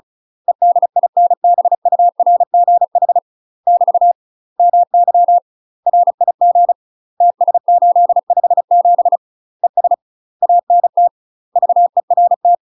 Morse code, heard as a keyed tone, about 36 wpm, 700 Hz. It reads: EDINBURGH <BT> MY RIG TS857 ES ANT VERT
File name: UK MASTER QSOs TRAINING / 35WPM / UK MASTER QSOs 0009